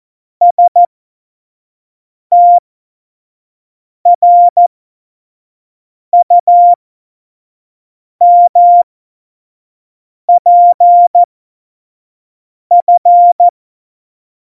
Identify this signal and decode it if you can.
Morse code, heard as a keyed tone, about 14 wpm, spaced as for 5 wpm, 700 Hz. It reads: STRUMPF
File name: Morsekurs / Witze / Witz 05. Juni 2018, 25 F70